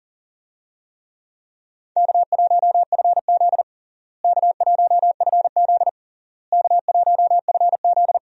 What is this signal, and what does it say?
Morse code, heard as a keyed tone, about 40 wpm, 700 Hz. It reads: K1FZ K1FZ K1FZ
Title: USA Call Signs - 4 Characters - Repeated 3x - 40wpm